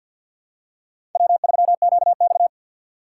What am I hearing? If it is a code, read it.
Morse code, heard as a keyed tone, about 49 wpm, 700 Hz. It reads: W3QX
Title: USA Call Signs - 3x Character Spacing - 50wpm